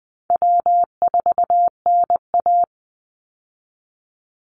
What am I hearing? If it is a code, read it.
Morse code, heard as a keyed tone, about 20 wpm, 700 Hz. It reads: W4NA
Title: Easy Call Signs - 20wpm